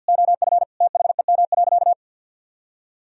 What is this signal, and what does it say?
Morse code, heard as a keyed tone, about 47 wpm, 700 Hz. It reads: OF THEM.
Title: Sentences from Top 1000 Words - Review Part 2 - 50wpm